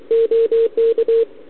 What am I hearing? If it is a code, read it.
Morse code, heard as a keyed tone, about 28 wpm, 440 Hz, with radio noise. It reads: OK
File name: ARCHIV / OKDX2020 / OK1RIb